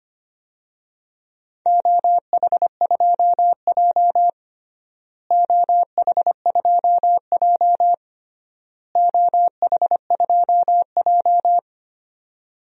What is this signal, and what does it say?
Morse code, heard as a keyed tone, about 25 wpm, 700 Hz. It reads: OH2J OH2J OH2J
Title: DXCC-100 Call Signs - 4 Characters - Repeated 3x - 25wpm